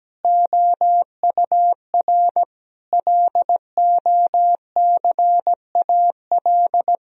Morse code, heard as a keyed tone, about 17 wpm, 700 Hz. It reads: OUR LOCAL
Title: Sentences from Top 1000 Words - Review Part 4 - 17wpm